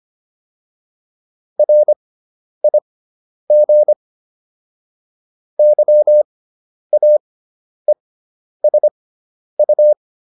Morse code, heard as a keyed tone, about 25 wpm, spaced as for 10 wpm, 600 Hz. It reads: RIG YAESU
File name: QSO407 10